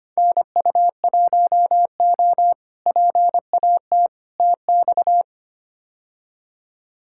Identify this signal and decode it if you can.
Morse code, heard as a keyed tone, about 25 wpm, 700 Hz. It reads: NU1O PAT TX
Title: K1USN SST QSO Practice - 25wpm